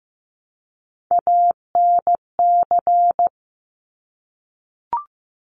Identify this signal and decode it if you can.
Morse code, heard as a keyed tone, about 15 wpm, 700 Hz. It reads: ANC
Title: Top Three Letter Combinations in Words 15wpm